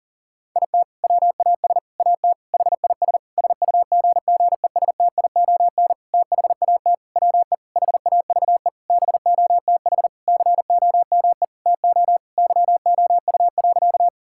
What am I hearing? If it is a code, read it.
Morse code, heard as a keyed tone, about 37 wpm, 700 Hz. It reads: IT WAS AT HIS SUGGESTION THAT WE HAVE BOTH COME TO YOU.
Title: Sentences from Top 1600 Words 40wpm